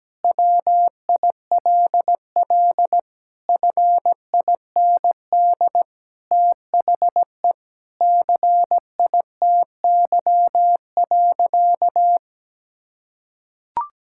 Morse code, heard as a keyed tone, about 17 wpm, 700 Hz. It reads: WILL FIND THE CITY.
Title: Sentences from Top 1000 Words - Review Part 6 - 17wpm